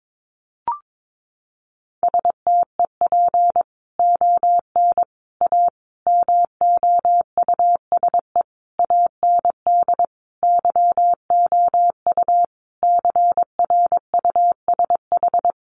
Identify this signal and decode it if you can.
Morse code, heard as a keyed tone, about 22 wpm, 700 Hz. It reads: STEP ON A MOUSE AND YOU CRUSH
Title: A Sound Of Thunder 22wpm